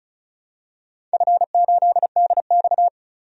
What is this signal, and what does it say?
Morse code, heard as a keyed tone, about 36 wpm, 700 Hz. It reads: F8DX
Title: DXCC-100 Call Signs - 4 Characters - 35wpm